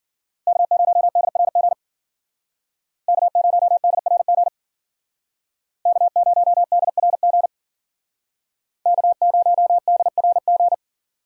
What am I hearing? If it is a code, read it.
Morse code, heard as a keyed tone, about 52 wpm, 700 Hz. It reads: K0DRG K0DRG K0DRG K0DRG